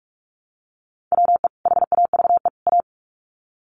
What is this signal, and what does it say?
Morse code, heard as a keyed tone, about 44 wpm, 700 Hz. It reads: WE HAVE A
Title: Sets of 3 Words 45wpm